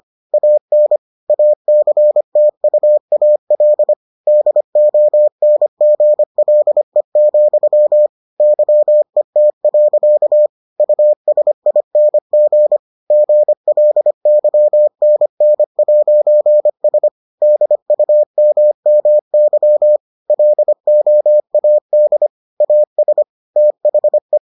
Morse code, heard as a keyed tone, about 25 wpm, 600 Hz. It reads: AN ACTUAL DONGLE, YET. USING GLYNN'S DUMMY LOAD AS THE